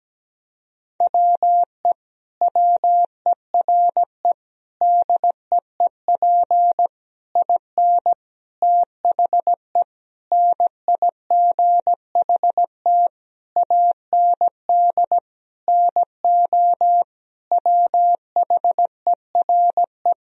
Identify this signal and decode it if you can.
Morse code, heard as a keyed tone, about 17 wpm, 700 Hz. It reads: WE WERE DEEP IN THE NIGHT AND NO WHERE
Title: Sentences from Top 1000 Words - Review Part 3 - 17wpm